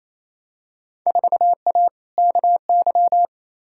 Morse code, heard as a keyed tone, about 28 wpm, 700 Hz. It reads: 4A KY
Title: Field Day Exchanges Encore - 28wpm